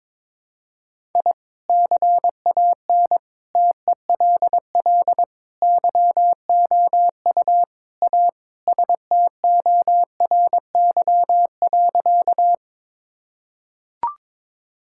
Morse code, heard as a keyed tone, about 22 wpm, 700 Hz. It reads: I CAN TELL YOU A STORY.
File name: practice / 22wpm / Sentences with 4-Word Phrases - 22wpm